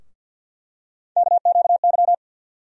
Morse code, heard as a keyed tone, about 52 wpm, 700 Hz. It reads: KQY
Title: USA Call Sign Postfixes - 3 Letters 50wpm